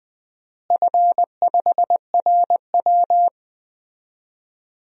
Morse code, heard as a keyed tone, about 20 wpm, 700 Hz. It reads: F5RW
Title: DXCC-100 Call Signs - 4 Characters - Repeated 3x - 20wpm